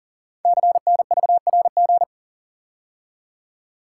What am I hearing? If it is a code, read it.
Morse code, heard as a keyed tone, about 42 wpm, 700 Hz. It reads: CNVRG
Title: Code Groups - 5 Letters - 40wpm